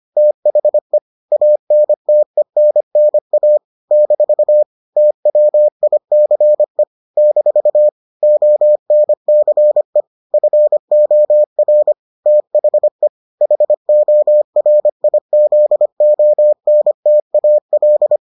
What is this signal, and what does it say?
Morse code, heard as a keyed tone, about 25 wpm, 600 Hz. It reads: THE ANTENNA - TWICE - ONCE FOR THE HORIZONTAL